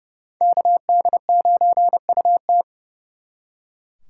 Morse code, heard as a keyed tone, about 37 wpm, 700 Hz. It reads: KD9UT
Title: Speed Racing - USA 2x2 Call Signs 30wpm